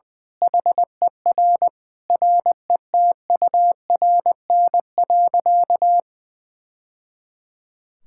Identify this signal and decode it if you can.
Morse code, heard as a keyed tone, about 20 wpm, 700 Hz. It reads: HER RETURN.
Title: Sentences from Top 1000 Words - Review Part 6 - 20wpm